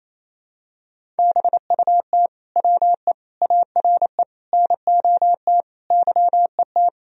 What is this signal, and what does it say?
Morse code, heard as a keyed tone, about 28 wpm, 700 Hz. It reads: BUT WE ARE NOT YET
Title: Sentences from Top 1000 Words - Review Part 1 - 28wpm